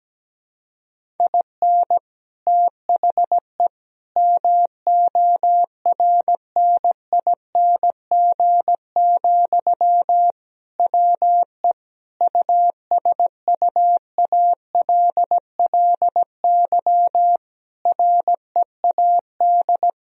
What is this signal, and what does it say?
Morse code, heard as a keyed tone, about 17 wpm, 700 Hz. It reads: IN THE MORNING, WE USUALLY READ
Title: Sentences from Top 1000 Words - Review Part 3 - 17wpm